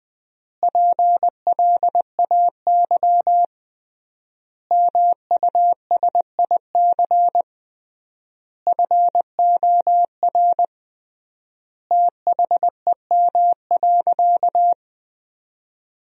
Morse code, heard as a keyed tone, about 20 wpm, 700 Hz. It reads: PLAY MUSIC FOR THEM.
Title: 3x Word-Spacing - Sentences from Top 300 Words - 20wpm